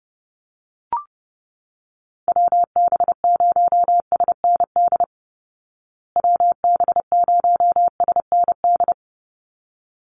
Morse code, heard as a keyed tone, about 30 wpm, 700 Hz. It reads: WB0SND WB0SND
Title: Call Signs Repeated 3x - 30wpm